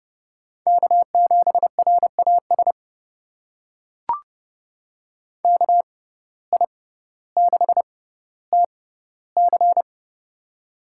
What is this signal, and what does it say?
Morse code, heard as a keyed tone, about 30 wpm, spaced as for 15 wpm, 700 Hz. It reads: K7RAS KI6TC
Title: USA Call Signs - 5 Characters - 6x Character Spacing - 30wpm